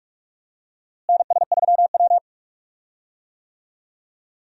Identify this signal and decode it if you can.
Morse code, heard as a keyed tone, about 46 wpm, 700 Hz. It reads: NS3W
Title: USA Call Signs - 4 Characters - Repeated 3x - 45wpm